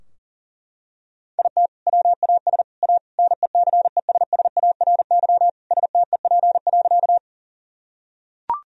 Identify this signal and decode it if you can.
Morse code, heard as a keyed tone, about 40 wpm, 700 Hz. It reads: IT WAS A NECESSARY STEP.
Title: Sentences from Top 1000 Words - Review Part 2 - 40wpm